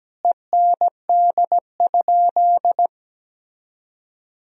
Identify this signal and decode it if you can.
Morse code, heard as a keyed tone, about 17 wpm, 700 Hz. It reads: END?
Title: Sentences from Top 1000 Words - Review Part 3 - 17wpm